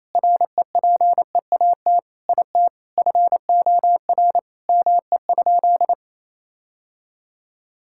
Morse code, heard as a keyed tone, about 28 wpm, 700 Hz. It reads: REPEAT IT FOR ME?
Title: Sentences from Top 1600 Words 28wpm